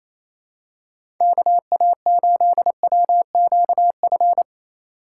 Morse code, heard as a keyed tone, about 27 wpm, 700 Hz. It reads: KA8WQF
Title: USA Call Signs - 6 Characters - 3x Character Spacing - 28wpm